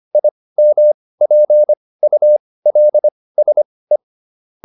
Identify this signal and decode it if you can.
Morse code, heard as a keyed tone, about 25 wpm, spaced as for 18 wpm, 600 Hz. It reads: IMPULSE